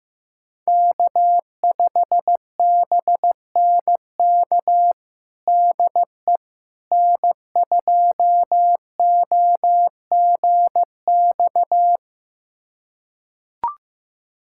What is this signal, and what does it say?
Morse code, heard as a keyed tone, about 15 wpm, 700 Hz. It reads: K5BNK DE N2OGX
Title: USA Callsign DE Callsign - 5 Characters - 15wpm